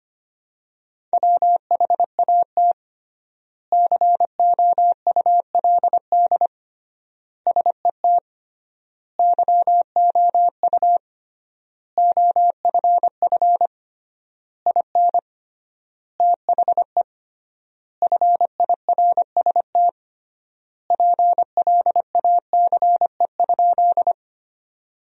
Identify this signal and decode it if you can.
Morse code, heard as a keyed tone, about 25 wpm, 700 Hz. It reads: WHAT COULD SET YOU OFF IN THE FIRST PLACE?
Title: 3x Word-Spacing - Sentences from Top 200 Words - 25wpm